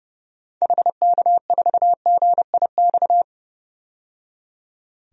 Morse code, heard as a keyed tone, about 30 wpm, 700 Hz. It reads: HK4GIX
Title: DXCC-100 Call Signs - 6 Characters - 30wpm